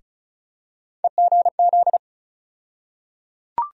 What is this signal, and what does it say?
Morse code, heard as a keyed tone, about 35 wpm, 700 Hz. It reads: EGZ
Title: USA Call Sign Postfixes - 3 Letters 35wpm